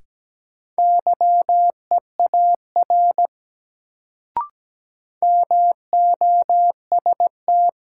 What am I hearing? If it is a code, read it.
Morse code, heard as a keyed tone, about 17 wpm, 700 Hz. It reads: YEAR MOST
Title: Sentences from Top 200 Words - New Word Drill - 17wpm